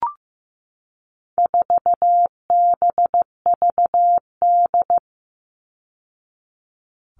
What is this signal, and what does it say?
Morse code, heard as a keyed tone, about 15 wpm, 700 Hz. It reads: 4BVD
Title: All Cut Numbers - 4-Digit 15wpm